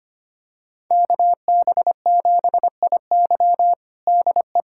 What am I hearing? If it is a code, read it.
Morse code, heard as a keyed tone, about 25 wpm, 700 Hz. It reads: KB7IY DE
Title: USA Callsign DE Callsign - 5 Characters - 25wpm